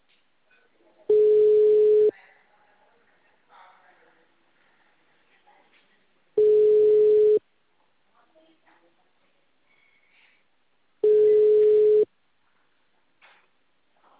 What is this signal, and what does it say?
Ringback tone (Europe, 425 Hz), with speech.